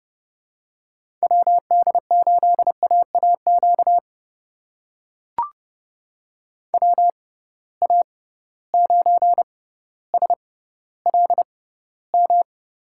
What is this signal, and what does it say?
Morse code, heard as a keyed tone, about 30 wpm, spaced as for 15 wpm, 700 Hz. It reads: WD8AAQ WA9SLM
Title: USA Call Signs - 6 Characters - 6x Character Spacing - 30wpm